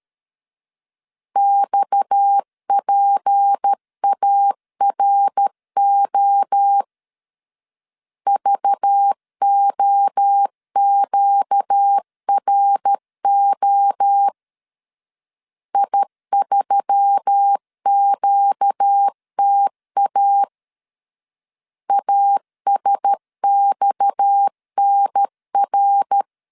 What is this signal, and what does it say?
Morse code, heard as a keyed tone, about 13 wpm, 800 Hz. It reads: XPARO VOQRO I3QTA ASXNR
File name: LEZIONI / 6 PKX0 / 6 PKX 12wpm 2024